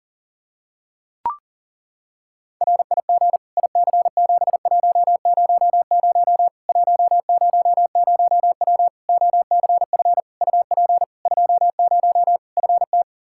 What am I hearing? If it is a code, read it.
Morse code, heard as a keyed tone, about 40 wpm, 700 Hz. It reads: RIG IC7100 100W OCF UP 20 FT
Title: Practice QSO - Exchange 2 - Info - 40wpm